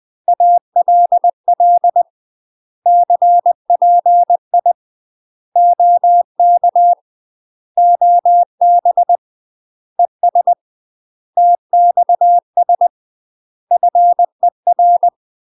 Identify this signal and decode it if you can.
Morse code, heard as a keyed tone, about 20 wpm, 700 Hz. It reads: ALL CPI OK OB ES TXS FER
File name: UK MASTER QSOs TRAINING / 20WPM / UK MASTER QSOs 0045